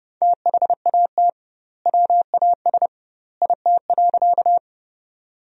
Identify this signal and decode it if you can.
Morse code, heard as a keyed tone, about 30 wpm, 700 Hz. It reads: THAT WAS IT.
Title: 2x Word-Spacing - Sentences from Top 100 Words - 30wpm